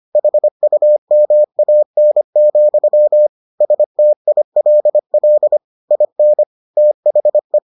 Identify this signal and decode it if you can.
Morse code, heard as a keyed tone, about 25 wpm, 600 Hz. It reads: HUMAN, STILL IN THE